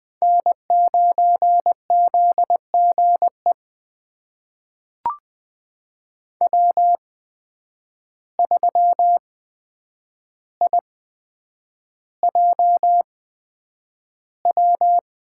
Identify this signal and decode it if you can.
Morse code, heard as a keyed tone, about 20 wpm, spaced as for 9 wpm, 700 Hz. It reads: N9ZGE W3IJW
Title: USA Call Signs - 5 Characters - 8x Character Spacing - 20wpm